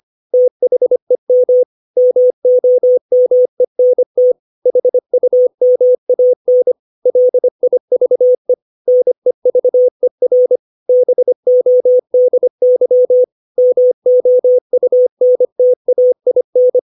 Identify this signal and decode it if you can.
Morse code, heard as a keyed tone, about 25 wpm, 500 Hz. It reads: THEM MOMENT HUMAN LIVE NEVER BODY MOUNTAIN
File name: 632 words / 632 Words 25WPM